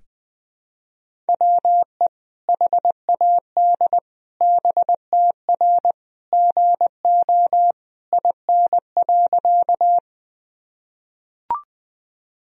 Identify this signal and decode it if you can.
Morse code, heard as a keyed tone, about 20 wpm, 700 Hz. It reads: WE HAD BTR GO IN.
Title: Sentences from Top 400 Words and Abbreviations 20wpm